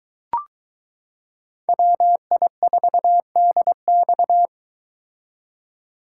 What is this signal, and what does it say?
Morse code, heard as a keyed tone, about 22 wpm, 700 Hz. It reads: WI4DX